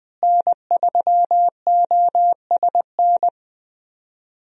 Morse code, heard as a keyed tone, about 20 wpm, 700 Hz. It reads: N3OSN
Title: USA Call Signs - 5 Characters - Repeated 2x - 20wpm